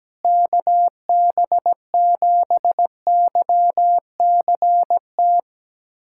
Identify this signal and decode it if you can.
Morse code, heard as a keyed tone, about 17 wpm, 700 Hz. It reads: KB7YCT
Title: USA Callsign DE Callsign 17wpm